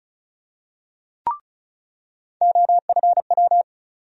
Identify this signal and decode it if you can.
Morse code, heard as a keyed tone, about 36 wpm, 700 Hz. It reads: OFW
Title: USA Call Sign Postfixes - 3 Letters 35wpm